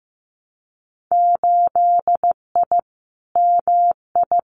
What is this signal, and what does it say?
Morse code, heard as a keyed tone, about 15 wpm, 700 Hz. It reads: 8I MI
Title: Winter Field Day Exchanges Encore 15wpm